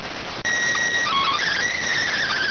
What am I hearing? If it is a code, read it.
SSTV (slow-scan television), PD 290.